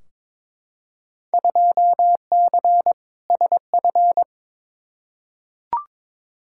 Morse code, heard as a keyed tone, about 22 wpm, 700 Hz. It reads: 2C SF
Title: Field Day Exchanges Encore - 22wpm